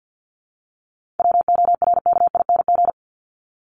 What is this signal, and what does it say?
Morse code, heard as a keyed tone, about 50 wpm, 700 Hz. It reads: WORKING
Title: Top 501-750 Words 50wpm